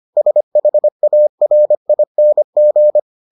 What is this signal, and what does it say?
Morse code, heard as a keyed tone, about 25 wpm, 600 Hz. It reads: SHARING